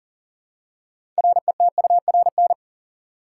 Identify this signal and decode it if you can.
Morse code, heard as a keyed tone, about 37 wpm, 700 Hz. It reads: RETURN
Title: Six Letter Words - 40wpm